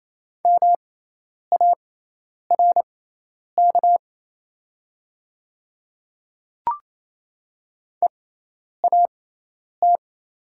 Morse code, heard as a keyed tone, about 27 wpm, spaced as for 10 wpm, 700 Hz. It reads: MARK EAT
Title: 6x Character Spacing - Top 300 Words - 28wpm